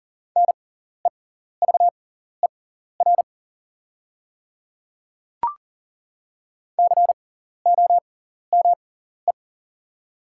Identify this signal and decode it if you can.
Morse code, heard as a keyed tone, about 37 wpm, spaced as for 14 wpm, 700 Hz. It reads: NEVER COME
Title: 6x Character Spacing - Top 300 Words - 40wpm